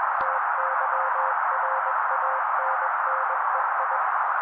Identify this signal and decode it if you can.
Morse code, heard as a keyed tone, about 21 wpm, 530 Hz. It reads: TYRANNEI
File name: daily / 20wpm noise